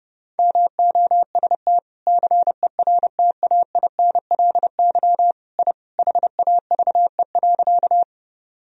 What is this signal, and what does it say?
Morse code, heard as a keyed tone, about 30 wpm, 700 Hz. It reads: MOST CERTAINLY I HAVE.
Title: Sentences from Top 1000 Words - Review Part 1 - 30wpm